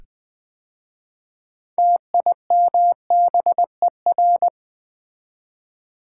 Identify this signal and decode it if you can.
Morse code, heard as a keyed tone, about 20 wpm, 700 Hz. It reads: TIMBER
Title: 6 Letter Words - First 3 Letters Given - 20wpm